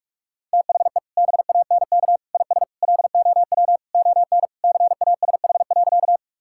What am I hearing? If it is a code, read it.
Morse code, heard as a keyed tone, about 48 wpm, 700 Hz. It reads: THE BANK IS LOW ON CASH.
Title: Sentences from Top 1000 Words - Review Part 5 - 45wpm